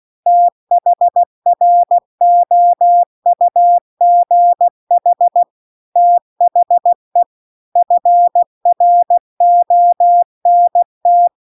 Morse code, heard as a keyed tone, about 16 wpm, 700 Hz. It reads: THROUGH THE FRONT